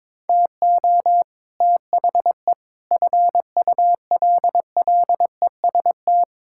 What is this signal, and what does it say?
Morse code, heard as a keyed tone, about 22 wpm, 700 Hz. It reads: TO THE FULLEST
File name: practice / 22wpm / Sets of 7 Words - 22wpm